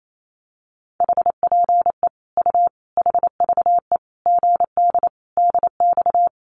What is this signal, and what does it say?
Morse code, heard as a keyed tone, about 28 wpm, 700 Hz. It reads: HPE U HVE GD DX
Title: Common QSO Phrases 28wpm